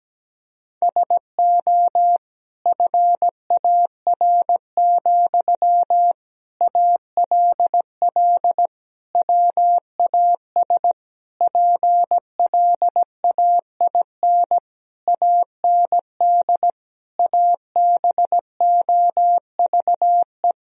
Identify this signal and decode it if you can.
Morse code, heard as a keyed tone, about 17 wpm, 700 Hz. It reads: SO FAR, ALL WAS PLAIN AND ABOVE
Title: Sentences from Top 1000 Words - Review Part 4 - 17wpm